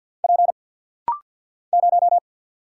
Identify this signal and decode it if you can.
Morse code, heard as a keyed tone, about 50 wpm, 700 Hz. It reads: P 0
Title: Single Letter-Number Alt - Rapid-Fire 50wpm